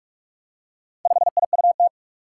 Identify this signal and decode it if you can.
Morse code, heard as a keyed tone, about 46 wpm, 700 Hz. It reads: 5IUT
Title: Four Challenging Letter-Number Combinations 45wpm